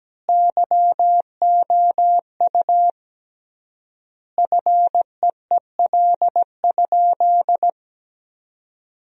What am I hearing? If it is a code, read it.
Morse code, heard as a keyed tone, about 17 wpm, 700 Hz. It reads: YOU FEEL?
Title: 3x Word-Spacing - Sentences from Top 400 Words - 17wpm